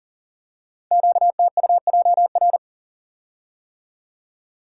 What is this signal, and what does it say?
Morse code, heard as a keyed tone, about 38 wpm, 700 Hz. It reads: QTUJR
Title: Code Groups - 5 Letters - 40wpm